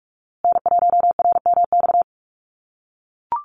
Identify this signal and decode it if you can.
Morse code, heard as a keyed tone, about 44 wpm, 700 Hz. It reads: N1RMX
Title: USA Call Signs Repeated 2x 45wpm